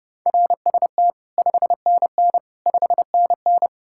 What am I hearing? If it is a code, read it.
Morse code, heard as a keyed tone, about 32 wpm, 700 Hz. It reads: RST 5NN 5NN